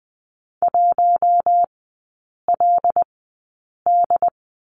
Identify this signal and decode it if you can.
Morse code, heard as a keyed tone, about 20 wpm, spaced as for 8 wpm, 700 Hz. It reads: 1LD
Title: 074 Lesson 13e - Letter L - Three Characters 20wpm